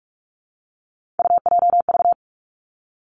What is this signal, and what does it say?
Morse code, heard as a keyed tone, about 46 wpm, 700 Hz. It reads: UJV
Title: Three Challenging Letter-Number Combinations 45wpm